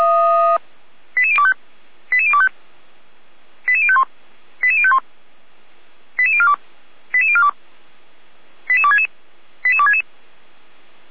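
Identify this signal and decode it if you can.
Five-tone selective call (ZVEI), heard as quick runs of five tones, tones of about 70 ms. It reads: 88025 88025 88051 88051 88042 88042 88266 88266
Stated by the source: ZVEI five-tone call 88235 received by SDR, channel 455, recorded by feuerwehr-orlamuende fire brigade server.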